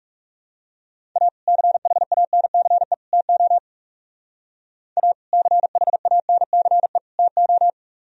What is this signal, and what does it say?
Morse code, heard as a keyed tone, about 35 wpm, 700 Hz. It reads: A CHANCE TO A CHANCE TO
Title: Speed-Racing - Sets of 3 Words 30wpm